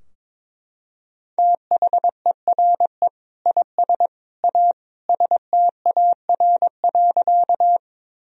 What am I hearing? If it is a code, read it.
Morse code, heard as a keyed tone, about 22 wpm, 700 Hz. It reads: THERE IS A STAR.
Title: Sentences from Top 1000 Words - Review Part 2 - 22wpm